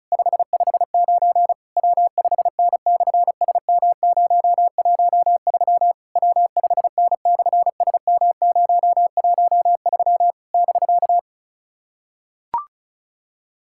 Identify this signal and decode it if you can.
Morse code, heard as a keyed tone, about 33 wpm, 700 Hz. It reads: 559 W5N/SM013 W5N/SM013 <BK>
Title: SOTA QSO Practice 35wpm